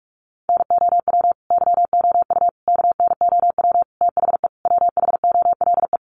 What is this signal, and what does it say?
Morse code, heard as a keyed tone, about 44 wpm, 700 Hz. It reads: NOW YOU KNOW THE WHOLE
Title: Sentences from Top 1200 Words 45wpm